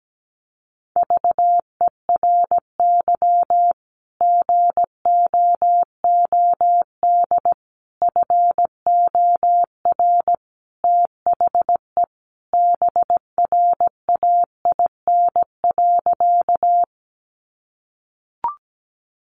Morse code, heard as a keyed tone, about 17 wpm, 700 Hz. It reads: VERY GOOD FOR THE BRAIN.
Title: Sentences from Top 1100 Words 17wpm